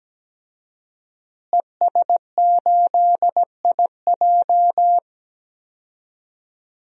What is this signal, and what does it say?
Morse code, heard as a keyed tone, about 17 wpm, 700 Hz. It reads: ES8IJ
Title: DXCC-100 Call Signs - 5 Characters - Repeated 2x - 17wpm